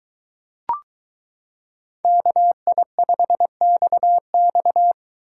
Morse code, heard as a keyed tone, about 23 wpm, 700 Hz. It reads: KI5XX